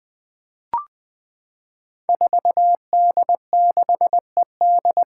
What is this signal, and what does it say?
Morse code, heard as a keyed tone, about 21 wpm, 700 Hz. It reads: 4D6ED
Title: Five Challenging Letter-Number Combinations 20wpm